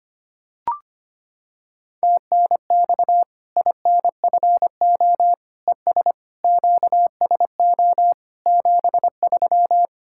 Morse code, heard as a keyed tone, about 25 wpm, 700 Hz. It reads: TNX INFO ES QSO 73
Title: Practice QSO - Exchange 4 - Ending - 25wpm